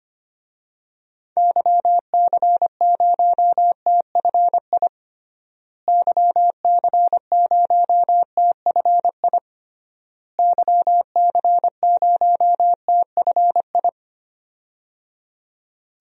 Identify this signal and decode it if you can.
Morse code, heard as a keyed tone, about 25 wpm, 700 Hz. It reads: YC0TFI YC0TFI YC0TFI
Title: DXCC-100 Call Signs - 6 Characters - Repeated 3x - 25wpm